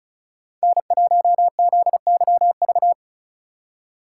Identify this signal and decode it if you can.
Morse code, heard as a keyed tone, about 43 wpm, 700 Hz. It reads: N1ZYV
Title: Speed Racing - USA Call Signs 35wpm